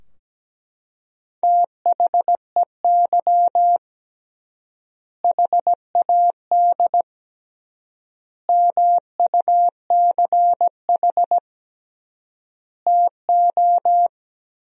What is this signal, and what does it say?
Morse code, heard as a keyed tone, about 17 wpm, 700 Hz. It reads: THEY HAD MUCH TO
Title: 3x Word-Spacing - Sentences from Top 400 Words - 17wpm